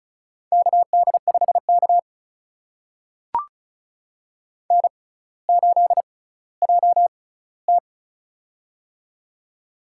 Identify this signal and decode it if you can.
Morse code, heard as a keyed tone, about 36 wpm, spaced as for 18 wpm, 700 Hz. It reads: KD5K N8JT
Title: USA Call Signs - 4 Characters - 6x Character Spacing - 35wpm